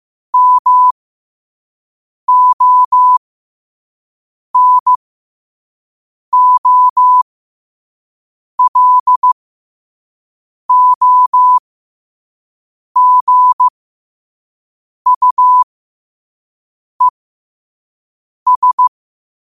Morse code, heard as a keyed tone, about 15 wpm, spaced as for 5 wpm, 1000 Hz. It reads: MONOLOGUES
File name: qotdfiles / 5wpm